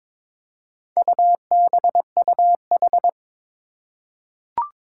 Morse code, heard as a keyed tone, about 21 wpm, 700 Hz. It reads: UBUH
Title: Four Challenging Letter-Number Combinations 22wpm